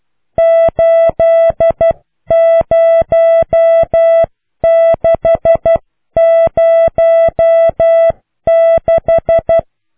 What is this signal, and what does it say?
Morse code, heard as a keyed tone, about 12 wpm, 660 Hz. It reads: 80606